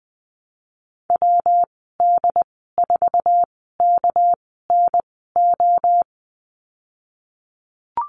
Morse code, heard as a keyed tone, about 20 wpm, spaced as for 15 wpm, 700 Hz. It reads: WD4KNO
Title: 238 Lesson 38g - Letter q - Call Signs 20wpm